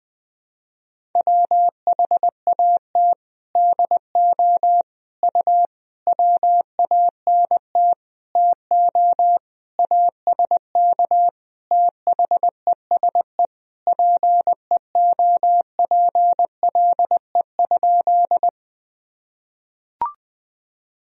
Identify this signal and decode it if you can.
Morse code, heard as a keyed tone, about 20 wpm, 700 Hz. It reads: WHAT DO U WANT TO ASK THESE PEOPLE?
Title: Sentences from Top 200 Words and Abbreviations 20wpm